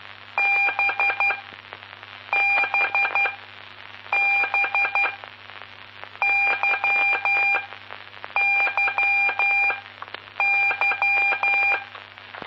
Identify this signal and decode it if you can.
Morse code, heard as a keyed tone, about 12 wpm, spaced as for 7 wpm, 2400 Hz, with noise. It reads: BBB YYY
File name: pp5bar / aulas de telegrafia / Aula-11